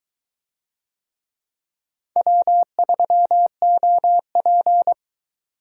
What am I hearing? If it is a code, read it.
Morse code, heard as a keyed tone, about 23 wpm, 700 Hz. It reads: W3OP